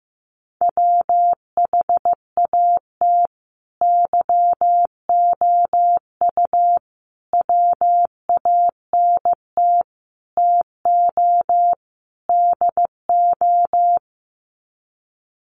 Morse code, heard as a keyed tone, about 15 wpm, 700 Hz. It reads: WHAT YOU WANT TO DO
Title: Sets of 5 Words Encore 15wpm